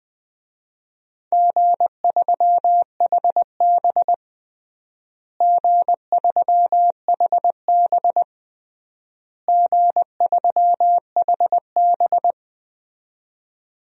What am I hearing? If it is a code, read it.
Morse code, heard as a keyed tone, about 20 wpm, 700 Hz. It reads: G3HB G3HB G3HB
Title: DXCC-100 Call Signs - 4 Characters - Repeated 3x - 20wpm